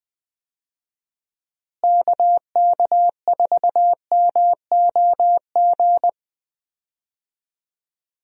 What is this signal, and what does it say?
Morse code, heard as a keyed tone, about 21 wpm, 700 Hz. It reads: KK4MOG
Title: USA Call Signs - 6 Characters - Repeated 3x - 20wpm